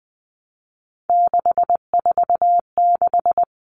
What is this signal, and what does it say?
Morse code, heard as a keyed tone, about 20 wpm, 700 Hz. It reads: 646
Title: Numbers - 3-Digit 20wpm